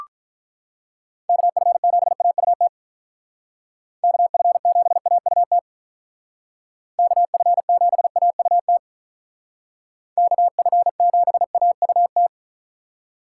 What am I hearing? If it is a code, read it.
Morse code, heard as a keyed tone, about 44 wpm, 700 Hz. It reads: KF7AUT KF7AUT KF7AUT KF7AUT
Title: Speed Racing - USA 2x3 Call Signs 35wpm